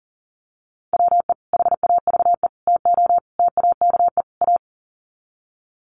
Morse code, heard as a keyed tone, about 40 wpm, 700 Hz. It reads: WE HAVE TO TAKE A
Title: Sets of 5 Words Encore 40wpm